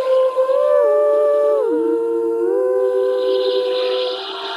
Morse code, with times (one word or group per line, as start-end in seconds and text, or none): none